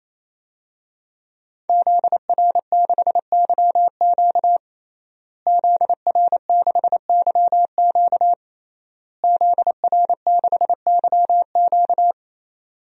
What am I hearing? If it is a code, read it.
Morse code, heard as a keyed tone, about 28 wpm, 700 Hz. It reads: ZR6YQ ZR6YQ ZR6YQ
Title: DXCC-100 Call Signs - 5 Characters - Repeated 3x - 28wpm